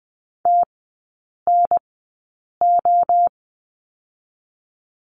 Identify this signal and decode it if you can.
Morse code, heard as a keyed tone, about 20 wpm, 700 Hz. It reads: T N O
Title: 029 Lesson 6e - Letter i - Three Characters 20wpm